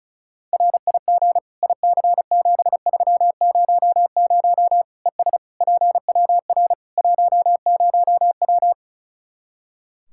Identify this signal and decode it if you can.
Morse code, heard as a keyed tone, about 36 wpm, 700 Hz. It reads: RIG IC7300 ES PWR 10W
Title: K1USN SST QSO Practice 35wpm